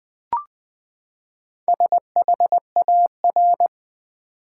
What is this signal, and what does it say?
Morse code, heard as a keyed tone, about 20 wpm, 700 Hz. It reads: SHAR
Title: Build A Word - 4 Letters At A Time - 11-Letter Words - 20wpm